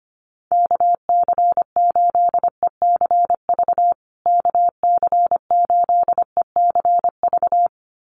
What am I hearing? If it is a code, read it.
Morse code, heard as a keyed tone, about 25 wpm, 700 Hz. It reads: KC8ECV KC8ECV
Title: USA Call Signs Repeated 2x 25wpm